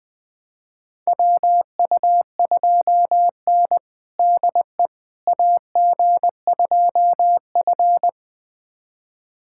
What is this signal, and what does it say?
Morse code, heard as a keyed tone, about 20 wpm, 700 Hz. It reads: WU2N DE AG2F
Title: USA Callsign DE Callsign - 4 Characters - 20wpm